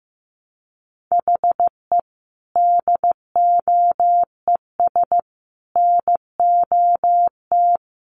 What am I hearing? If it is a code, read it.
Morse code, heard as a keyed tone, about 15 wpm, 700 Hz. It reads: HE DOES NOT
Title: Sets of 5 Words Encore 15wpm